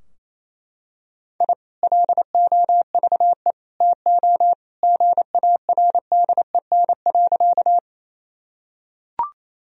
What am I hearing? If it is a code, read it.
Morse code, heard as a keyed tone, about 28 wpm, 700 Hz. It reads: I LOVE TO GARDEN.
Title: Sentences from Top 1000 Words - Review Part 4 - 28wpm